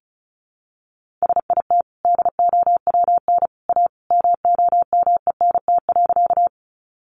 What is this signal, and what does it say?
Morse code, heard as a keyed tone, about 34 wpm, 700 Hz. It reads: SIT DOWN A MOMENT.
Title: Sentences from Top 1000 Words 35wpm